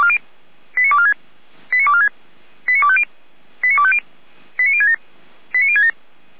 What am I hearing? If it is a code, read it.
Five-tone selective call (ZVEI), heard as quick runs of five tones, tones of about 80 ms. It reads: 89246 89246 89250 89250 89067 89067